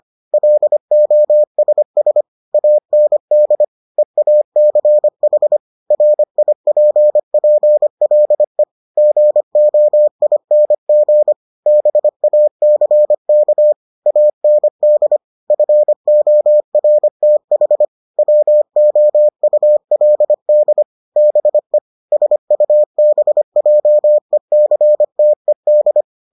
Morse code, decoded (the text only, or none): LOSS AND EACH RIPPLE GOING BACK AND FORTH WOULD BE SUBJECTED